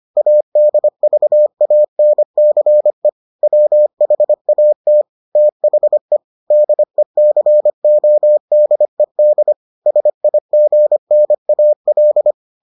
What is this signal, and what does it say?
Morse code, heard as a keyed tone, about 25 wpm, 600 Hz. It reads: ADVANCE WHAT THE DECODED SIGNAL